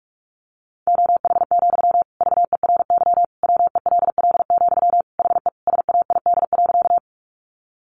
Morse code, heard as a keyed tone, about 44 wpm, 700 Hz. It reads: OH, VERY WELL, HE SAID.
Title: Sentences from Top 500 Words 45wpm